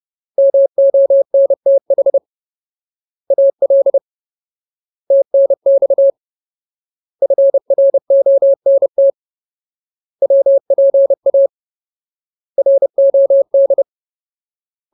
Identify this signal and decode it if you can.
Morse code, heard as a keyed tone, about 30 wpm, 550 Hz. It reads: MONTH AL TNX FRONT WPA ROD
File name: Ultimate Word List / 30WPM Ultimate